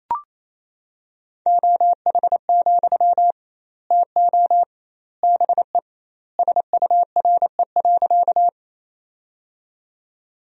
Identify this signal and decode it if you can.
Morse code, heard as a keyed tone, about 28 wpm, 700 Hz. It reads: OH, TO BE SURE.
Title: 2x Word-Spacing - Sentences from Top 500 Words - 28wpm